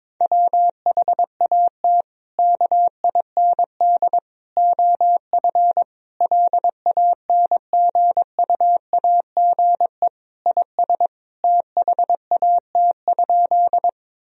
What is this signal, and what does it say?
Morse code, heard as a keyed tone, about 22 wpm, 700 Hz. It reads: WHAT KIND OF LANGUAGE IS THAT?
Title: Sentences from Top 1000 Words - Review Part 3 - 22wpm